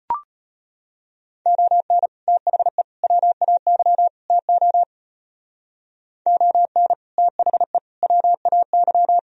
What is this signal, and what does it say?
Morse code, heard as a keyed tone, about 30 wpm, 700 Hz. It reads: ON THE WAY TO ON THE WAY
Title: Speed-Racing - Sets of 4 Words 25wpm